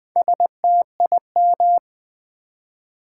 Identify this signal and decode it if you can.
Morse code, heard as a keyed tone, about 20 wpm, 700 Hz. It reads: STIM
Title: Build A Word - 2 Letters At A Time - 8-Letter Words - 20wpm